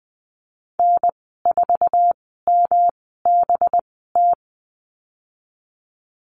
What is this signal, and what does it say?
Morse code, heard as a keyed tone, about 20 wpm, spaced as for 14 wpm, 700 Hz. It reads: N4MBT
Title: 186 Lesson 30g - Letter b - Call Signs 20wpm